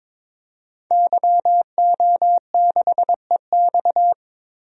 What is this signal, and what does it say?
Morse code, heard as a keyed tone, about 22 wpm, 700 Hz. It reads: YO6EX
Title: DXCC-100 Call Signs - 5 Characters - Repeated 3x - 22wpm